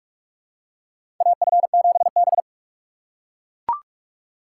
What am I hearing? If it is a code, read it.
Morse code, heard as a keyed tone, about 44 wpm, 700 Hz. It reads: AF7B
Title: USA Call Signs - 4 Characters - 4x Character Spacing - 45wpm